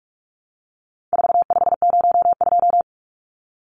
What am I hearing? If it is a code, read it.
Morse code, heard as a keyed tone, about 44 wpm, 700 Hz. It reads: 4502
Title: Numbers - 4-Digit 45wpm